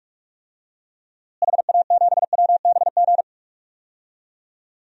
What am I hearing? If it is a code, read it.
Morse code, heard as a keyed tone, about 42 wpm, 700 Hz. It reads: HA7WBG
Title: DXCC-100 Call Signs - 6 Characters - Repeated 2x - 45wpm